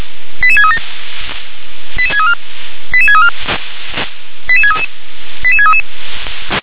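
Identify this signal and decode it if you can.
Five-tone selective call (ZVEI), heard as quick runs of five tones, tones of about 80 ms. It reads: 88637 88634 88634 88633 88633
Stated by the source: ZVEI five-tone call received on an SDR.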